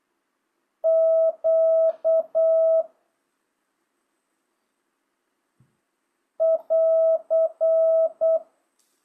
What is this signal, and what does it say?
Morse code, heard as a keyed tone, about 8 wpm, 650 Hz. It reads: Q <AR>